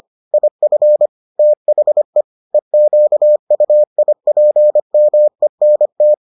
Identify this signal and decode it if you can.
Morse code, heard as a keyed tone, about 25 wpm, 600 Hz. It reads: IF THE EQUIPMENT